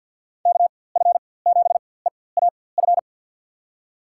Morse code, heard as a keyed tone, about 48 wpm, spaced as for 23 wpm, 700 Hz. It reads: KF7EAF
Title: USA Call Signs - 4x Character Spacing - 50wpm